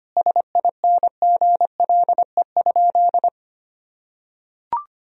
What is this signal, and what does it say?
Morse code, heard as a keyed tone, about 25 wpm, 700 Hz. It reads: SINGLE?
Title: Sentences from Top 1000 Words - Review Part 4 - 25wpm